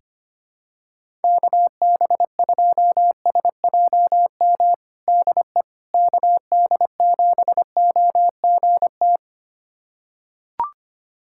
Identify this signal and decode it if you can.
Morse code, heard as a keyed tone, about 25 wpm, 700 Hz. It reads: KB2SJM DE KD7OGT
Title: USA Callsign DE Callsign - 6 Characters - 25wpm